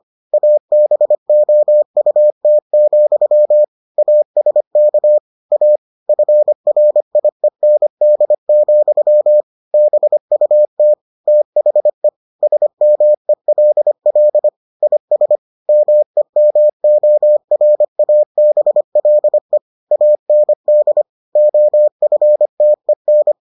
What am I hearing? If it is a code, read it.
Morse code, heard as a keyed tone, about 25 wpm, 600 Hz. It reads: ABOUT, ASK A FRIEND, BUT THE SMELL IS MEMORABLE AND OFTEN